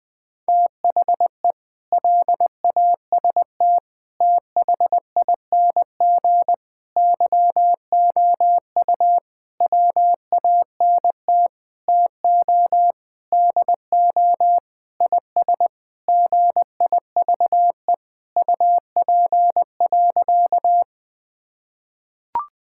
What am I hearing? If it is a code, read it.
Morse code, heard as a keyed tone, about 20 wpm, 700 Hz. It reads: THE LAST THING YOU WANT TO DO IS GIVE UP.
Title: Sentences with 7-Word Phrases - Spoken Beginning - 20wpm